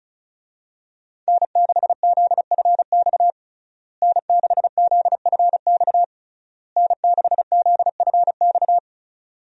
Morse code, heard as a keyed tone, about 36 wpm, 700 Hz. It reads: N6ZFX N6ZFX N6ZFX
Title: USA Call Signs - 5 Characters - Repeated 3x - 35wpm